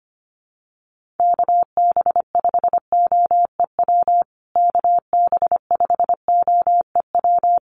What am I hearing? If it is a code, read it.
Morse code, heard as a keyed tone, about 25 wpm, 700 Hz. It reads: KB5OEW KB5OEW
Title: USA Call Signs Repeated 3x 25wpm